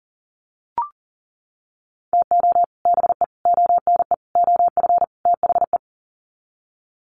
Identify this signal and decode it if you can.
Morse code, heard as a keyed tone, about 40 wpm, 700 Hz. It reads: TO BE ONE OF THE
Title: Sets of 5 Words 40wpm